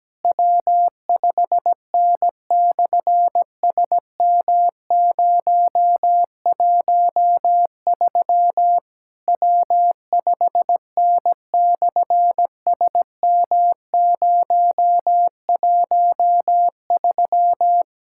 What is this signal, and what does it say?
Morse code, heard as a keyed tone, about 17 wpm, 700 Hz. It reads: W5N/SM013 W5N/SM013
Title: SOTA QSO Practice 17wpm